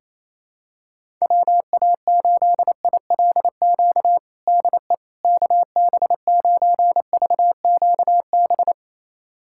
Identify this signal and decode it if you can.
Morse code, heard as a keyed tone, about 27 wpm, 700 Hz. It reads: WA8ILQ DE KB9VQB
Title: USA Callsign DE Callsign 28wpm